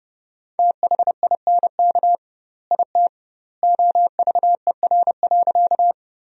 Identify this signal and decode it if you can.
Morse code, heard as a keyed tone, about 30 wpm, 700 Hz. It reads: THINK IT OVER.
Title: 2x Word-Spacing - Sentences from Top 200 Words - 30wpm